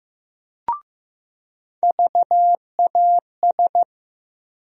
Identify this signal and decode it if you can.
Morse code, heard as a keyed tone, about 15 wpm, 700 Hz. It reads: VAS
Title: USA Call Sign Postfixes - 3 Letters 15wpm